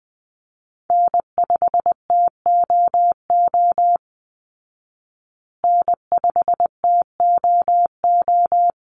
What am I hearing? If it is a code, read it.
Morse code, heard as a keyed tone, about 20 wpm, 700 Hz. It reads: N5TOO N5TOO
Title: Call Signs Repeated 3x - 20wpm